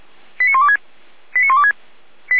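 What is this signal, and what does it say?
Five-tone selective call (ZVEI), heard as quick runs of five tones, tones of about 70 ms. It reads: 89126 89126
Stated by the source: ZVEI five-tone selective call 89208, received on an SDR.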